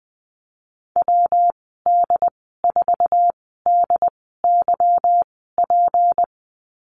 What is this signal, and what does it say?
Morse code, heard as a keyed tone, about 20 wpm, spaced as for 15 wpm, 700 Hz. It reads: WD4DYP
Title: 193 Lesson 31g - Letter v - Call Signs 20wpm